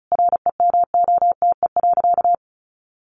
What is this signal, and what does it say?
Morse code, heard as a keyed tone, about 34 wpm, 700 Hz. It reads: REMOTE.
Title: Sentences from Top 1300 Words 35wpm